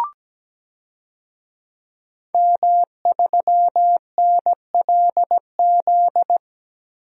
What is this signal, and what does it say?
Morse code, heard as a keyed tone, about 17 wpm, 700 Hz. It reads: M3NLZ
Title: DXCC-100 Call Signs - 5 Characters - 17wpm